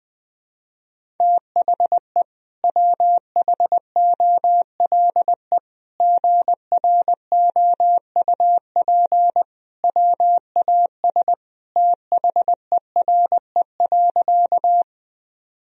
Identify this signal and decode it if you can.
Morse code, heard as a keyed tone, about 21 wpm, 700 Hz. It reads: THE WHOLE GROUP WAS THERE.
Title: Sentences from Top 1000 Words - Review Part 1 - 20wpm